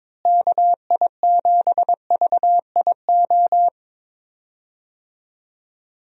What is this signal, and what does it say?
Morse code, heard as a keyed tone, about 23 wpm, 700 Hz. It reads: KI7VIO
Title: USA Call Signs - 6 Characters - Repeated 3x - 22wpm